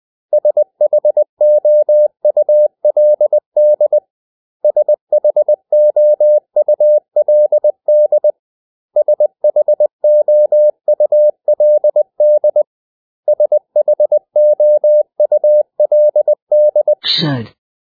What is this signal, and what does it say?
Morse code, heard as a keyed tone, about 20 wpm, 600 Hz. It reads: SHOULD SHOULD SHOULD SHOULD
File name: cw / 100 Common English Words